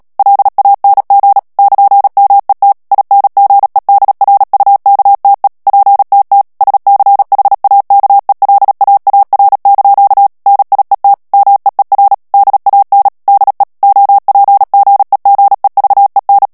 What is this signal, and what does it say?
Morse code, heard as a keyed tone, about 37 wpm, 800 Hz. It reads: LANG <KN>MET INGEDRUKTE PTT SCHAKELAAR) NIET MEER DAN DE OPGEGEVEN
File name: morse mp3 / HOLLANDAIS texte clair 2 vitesse 35 mots minute